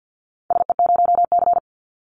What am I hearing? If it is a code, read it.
Morse code, heard as a keyed tone, about 48 wpm, 700 Hz. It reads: SE0C
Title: Call Signs Repeated 2x - 50wpm